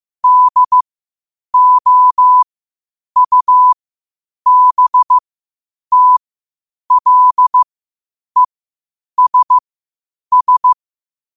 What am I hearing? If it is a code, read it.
Morse code, heard as a keyed tone, about 15 wpm, spaced as for 8 wpm, 1000 Hz. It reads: DOUBTLESS
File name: qotdfiles / 8wpm